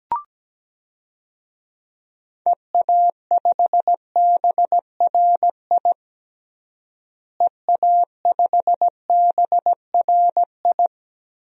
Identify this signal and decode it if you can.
Morse code, heard as a keyed tone, about 17 wpm, 700 Hz. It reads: EA5BRI EA5BRI
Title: DXCC-100 Call Signs - 6 Characters - Repeated 3x - 17wpm